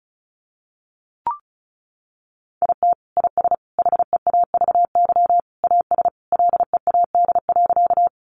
Morse code, heard as a keyed tone, about 34 wpm, 700 Hz. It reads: IT IS HEAVY AS LEAD.
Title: Sentences from Top 1100 Words 35wpm